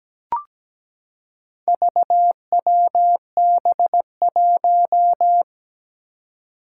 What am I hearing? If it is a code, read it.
Morse code, heard as a keyed tone, about 17 wpm, 700 Hz. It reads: VWB1
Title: Four Challenging Letter-Number Combinations 17wpm